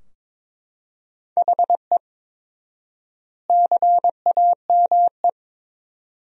Morse code, heard as a keyed tone, about 21 wpm, 700 Hz. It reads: HE CAME
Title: 4x Word-Spacing - Sentences from Top 300 Words - 22wpm